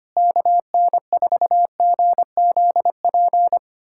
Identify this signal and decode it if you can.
Morse code, heard as a keyed tone, about 25 wpm, 700 Hz. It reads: KN4GZP
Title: USA Callsign DE Callsign - 6 Characters - 25wpm